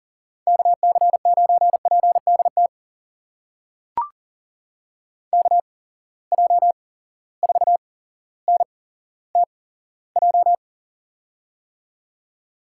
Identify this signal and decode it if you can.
Morse code, heard as a keyed tone, about 40 wpm, spaced as for 17 wpm, 700 Hz. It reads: KC9PDT KJ4NTJ
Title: USA Call Signs - 8x Character Spacing - 40wpm